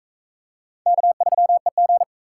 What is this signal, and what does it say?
Morse code, heard as a keyed tone, about 36 wpm, 700 Hz. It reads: K3EG